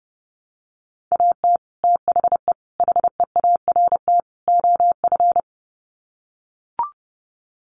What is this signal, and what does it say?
Morse code, heard as a keyed tone, about 30 wpm, 700 Hz. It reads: AT THE HEART OF
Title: Sets of 4 Words 30wpm